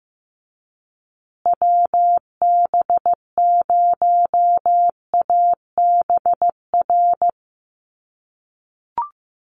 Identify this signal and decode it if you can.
Morse code, heard as a keyed tone, about 15 wpm, 700 Hz. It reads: WB0ABR
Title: USA Call Signs Repeated 3x 15wpm